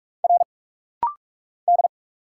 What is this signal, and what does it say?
Morse code, heard as a keyed tone, about 46 wpm, 700 Hz. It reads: R D
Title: Single Letters - Rapid-Fire 45wpm